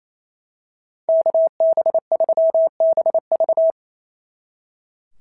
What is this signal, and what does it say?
Morse code, heard as a keyed tone, about 28 wpm, 650 Hz. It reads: KB3BV
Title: SOTA QSO Practice 28wpm